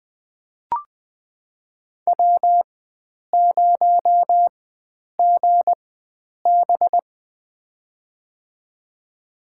Morse code, heard as a keyed tone, about 20 wpm, spaced as for 9 wpm, 700 Hz. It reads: W0GB
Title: USA Call Signs - 4 Characters - 4x Character Spacing - 20wpm